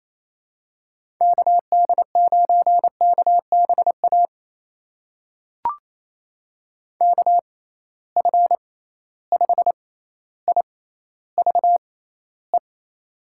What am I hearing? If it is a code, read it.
Morse code, heard as a keyed tone, about 28 wpm, spaced as for 14 wpm, 700 Hz. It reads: KD9KBA KF5IVE
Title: USA Call Signs - 6 Characters - 6x Character Spacing - 28wpm